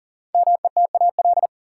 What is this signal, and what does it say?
Morse code, heard as a keyed tone, about 40 wpm, 700 Hz. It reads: METAL
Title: Sentences from Top 1000 Words - Review Part 6 - New Word Drill - 40wpm